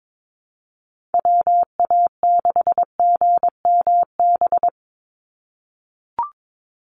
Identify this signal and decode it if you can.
Morse code, heard as a keyed tone, about 22 wpm, 700 Hz. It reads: WA6GMB
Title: USA Call Signs 22wpm